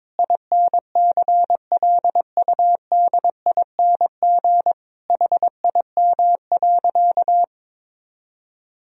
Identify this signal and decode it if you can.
Morse code, heard as a keyed tone, about 22 wpm, 700 Hz. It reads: INCLUDING HIM.
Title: Sentences from Top 1000 Words - Review Part 4 - 22wpm